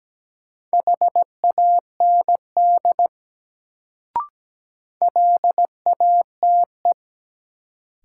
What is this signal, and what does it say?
Morse code, heard as a keyed tone, about 17 wpm, 700 Hz. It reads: HAND LATE
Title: Sets of 3 Words - New Word Drill - 17wpm